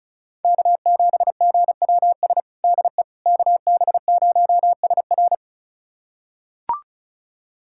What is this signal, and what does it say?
Morse code, heard as a keyed tone, about 36 wpm, 700 Hz. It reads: K7GWS DE KB0SR
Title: USA Callsign DE Callsign - 5 Characters - 35wpm